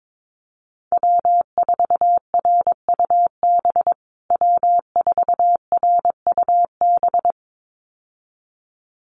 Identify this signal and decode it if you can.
Morse code, heard as a keyed tone, about 22 wpm, 700 Hz. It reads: W4RUB W4RUB
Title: USA Call Signs Repeated 2x 22wpm